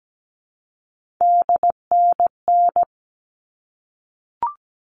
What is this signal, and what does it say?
Morse code, heard as a keyed tone, about 17 wpm, 700 Hz. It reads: DNN
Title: All Cut Numbers - 3-Digit 17wpm